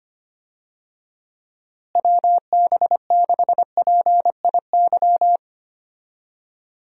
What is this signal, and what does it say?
Morse code, heard as a keyed tone, about 25 wpm, 700 Hz. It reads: WB6PIY
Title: USA Call Signs - 6 Characters - Repeated 2x - 25wpm